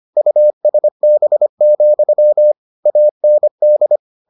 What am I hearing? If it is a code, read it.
Morse code, heard as a keyed tone, about 25 wpm, 600 Hz. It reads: USB, AND